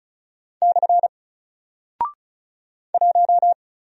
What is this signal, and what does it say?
Morse code, heard as a keyed tone, about 34 wpm, 700 Hz. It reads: / 1
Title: Single Character (Letter:Number:Punctuation:Prosign) 35wpm